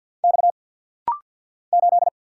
Morse code, heard as a keyed tone, about 55 wpm, 700 Hz. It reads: X 8
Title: Single Letter-Number Alt - Rapid-Fire 50wpm